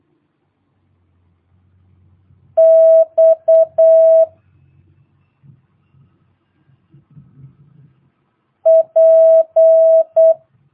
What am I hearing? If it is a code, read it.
Morse code, heard as a keyed tone, about 8 wpm, 650 Hz. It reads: X P
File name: jh8cbh / 2022cw